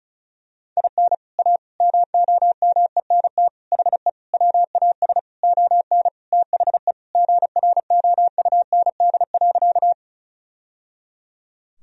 Morse code, heard as a keyed tone, about 34 wpm, 700 Hz. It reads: IN A MOMENT HE WAS ON THE GROUND.
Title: Sentences from Top 1000 Words - Review Part 2 - 35wpm